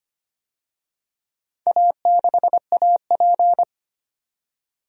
Morse code, heard as a keyed tone, about 25 wpm, 700 Hz. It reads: A6AP
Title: DXCC-100 Call Signs - 4 Characters - 25wpm